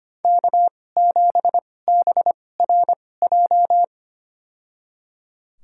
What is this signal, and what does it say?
Morse code, heard as a keyed tone, about 25 wpm, spaced as for 18 wpm, 700 Hz. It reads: K7BRJ
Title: USA Call Signs - 5 Characters - 2x Character Spacing - 25wpm